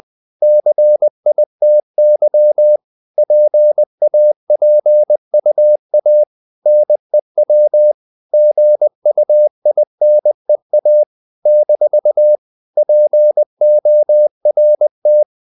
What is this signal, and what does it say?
Morse code, heard as a keyed tone, about 20 wpm, 600 Hz. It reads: CITY PAPUA NEW GUINEA - PORT
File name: CW / Practice / NationalCapitals20